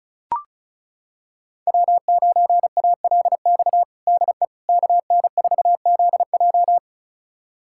Morse code, heard as a keyed tone, about 36 wpm, 700 Hz. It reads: W9ALX DE KN4ZJ
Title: USA Callsign DE Callsign - 5 Characters - 35wpm